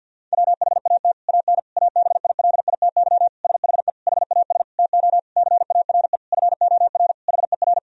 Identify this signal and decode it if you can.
Morse code, heard as a keyed tone, about 52 wpm, 700 Hz. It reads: WHAT AN ABILITY SHE HAS TO CARE FOR HER
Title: Sentences from Top 1000 Words - Review Part 3 - 50wpm